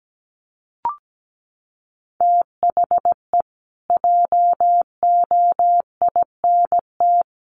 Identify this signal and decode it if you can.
Morse code, heard as a keyed tone, about 17 wpm, 700 Hz. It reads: THE JOINT
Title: Sets of 5 Words Encore2 17wpm